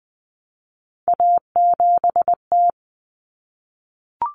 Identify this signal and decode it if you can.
Morse code, heard as a keyed tone, about 20 wpm, 700 Hz. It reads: A7T
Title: Common Cut Numbers - 3-Digit 20wpm